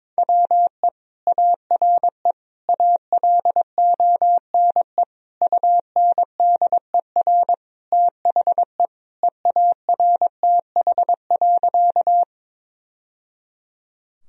Morse code, heard as a keyed tone, about 21 wpm, 700 Hz. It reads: WE ARE ALONE UNDER THE EARTH.
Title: Sentences from Top 1000 Words - Review Part 1 - 22wpm